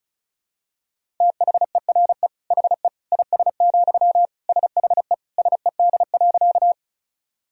Morse code, heard as a keyed tone, about 36 wpm, 700 Hz. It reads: THERE HE IS, SHE SED.
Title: Sentences from Top 100 Words and Abbreviations 35wpm